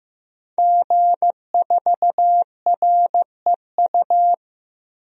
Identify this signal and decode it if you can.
Morse code, heard as a keyed tone, about 15 wpm, 700 Hz. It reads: G4REU
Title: DXCC-100 Call Signs - 5 Characters - 15wpm